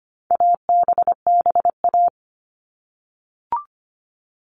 Morse code, heard as a keyed tone, about 25 wpm, 700 Hz. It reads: ABBA
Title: All Cut Numbers - 4-Digit 25wpm